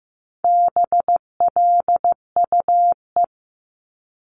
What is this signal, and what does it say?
Morse code, heard as a keyed tone, about 15 wpm, 700 Hz. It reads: BLUE
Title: Multi-letter Phonemes 15wpm